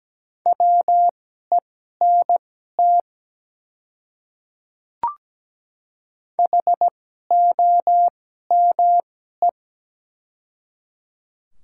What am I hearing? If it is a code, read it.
Morse code, heard as a keyed tone, about 17 wpm, spaced as for 12 wpm, 700 Hz. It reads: WENT HOME
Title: 2x Character Spacing - Top 200 Words - 17wpm